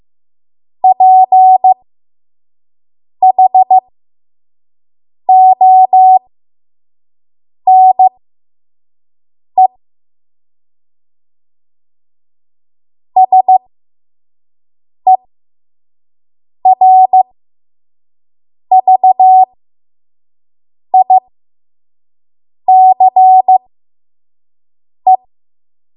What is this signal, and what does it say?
Morse code, heard as a keyed tone, about 15 wpm, spaced as for 5 wpm, 750 Hz. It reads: PHONE SERVICE
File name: Morse / 05WPM